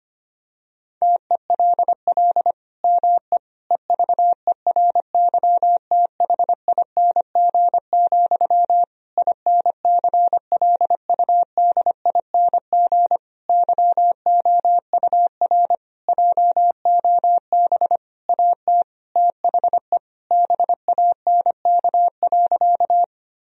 Morse code, heard as a keyed tone, about 25 wpm, 700 Hz. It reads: TELL ME EVERYTHING, INCLUDING YOUR JOB AT THE BANK.
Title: Sentences from Top 1000 Words - Review Part 4 - 25wpm